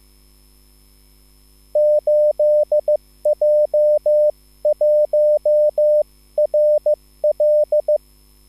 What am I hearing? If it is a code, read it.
Morse code, heard as a keyed tone, about 15 wpm, 600 Hz. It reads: 8J1RL
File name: QSL / CW lesson